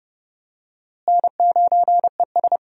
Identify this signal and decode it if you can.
Morse code, heard as a keyed tone, about 30 wpm, 700 Hz. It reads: N9ES
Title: USA Call Signs - 4 Characters - Repeated 3x - 30wpm